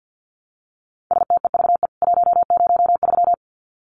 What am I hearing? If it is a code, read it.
Morse code, heard as a keyed tone, about 48 wpm, 700 Hz. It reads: STEVE 103
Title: CWT Exchange 50wpm